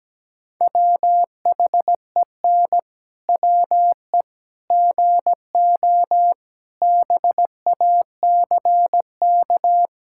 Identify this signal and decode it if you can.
Morse code, heard as a keyed tone, about 17 wpm, 700 Hz. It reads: WHEN WE GO BACK
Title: Pangram Challenge v2 - 17wpm